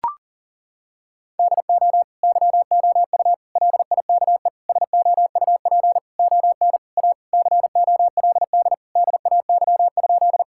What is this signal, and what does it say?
Morse code, heard as a keyed tone, about 40 wpm, 700 Hz. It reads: DO YOU LIKE SOUP ON A COLD DAY?
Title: Sentences from Top 1000 Words - Review Part 6 - 40wpm